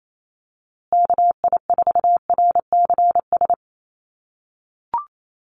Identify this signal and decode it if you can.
Morse code, heard as a keyed tone, about 28 wpm, 700 Hz. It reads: KI4RCS
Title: USA Call Signs Repeated 2x 28wpm